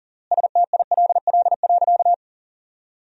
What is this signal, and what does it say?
Morse code, heard as a keyed tone, about 40 wpm, 700 Hz. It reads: STILL.
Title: 3x Word-Spacing - Sentences from Top 500 Words - 40wpm